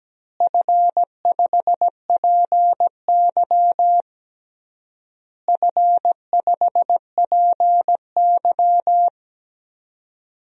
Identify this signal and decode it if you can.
Morse code, heard as a keyed tone, about 17 wpm, 700 Hz. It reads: F5PY F5PY
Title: DXCC-100 Call Signs - 4 Characters - Repeated 3x - 17wpm